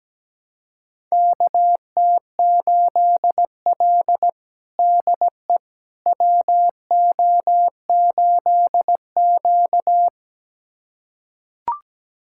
Morse code, heard as a keyed tone, about 17 wpm, 700 Hz. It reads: KT8L DE WO8Q
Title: USA Callsign DE Callsign - 4 Characters - 17wpm